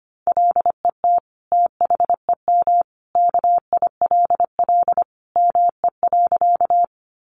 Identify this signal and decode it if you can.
Morse code, heard as a keyed tone, about 25 wpm, 700 Hz. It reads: LET THEM KILL ME.
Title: Sentences from Top 1400 Words 25wpm